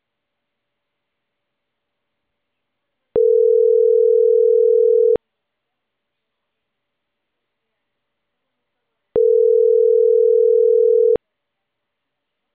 Ringback tone (North America), 440 + 480 Hz.